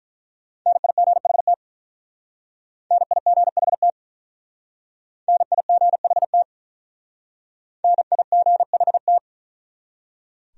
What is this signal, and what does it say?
Morse code, heard as a keyed tone, about 43 wpm, 700 Hz. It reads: NIGHT NIGHT NIGHT NIGHT
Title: Speed-Racing Top 500 Words 35wpm